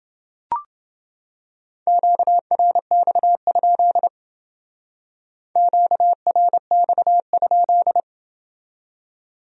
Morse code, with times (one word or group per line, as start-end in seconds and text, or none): 1.9-4.1 QRX?
5.6-8.0 QRX?